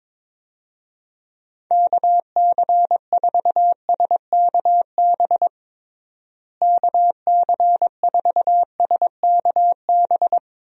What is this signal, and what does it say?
Morse code, heard as a keyed tone, about 22 wpm, 700 Hz. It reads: KC4SKB KC4SKB
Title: USA Call Signs - 6 Characters - Repeated 2x - 22wpm